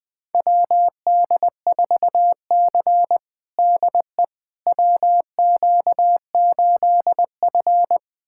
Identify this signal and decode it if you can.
Morse code, heard as a keyed tone, about 20 wpm, 700 Hz. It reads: WD4C DE WQ8F
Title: USA Callsign DE Callsign - 4 Characters - 20wpm